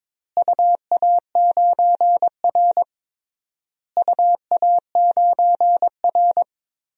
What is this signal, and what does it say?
Morse code, heard as a keyed tone, about 22 wpm, 700 Hz. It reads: UA9R UA9R
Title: DXCC-100 Call Signs - 4 Characters - Repeated 3x - 22wpm